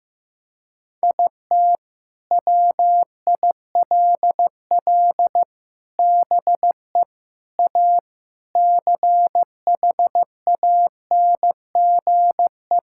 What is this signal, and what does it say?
Morse code, heard as a keyed tone, about 16 wpm, 700 Hz. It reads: IT WILL BE A CHANGE
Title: Sentences from Top 1000 Words - Review Part 6 - 15wpm